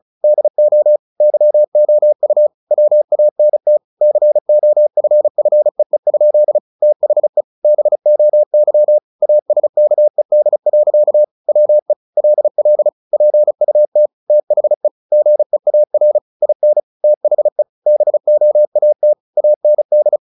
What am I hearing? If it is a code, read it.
Morse code, heard as a keyed tone, about 35 wpm, 600 Hz. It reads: DO YOU WANT COFFEE? THE BOY ASKED. WE LL PUT THE GEAR IN THE BOAT AND
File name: the old man and the sea / the old man and the sea 1 35wpm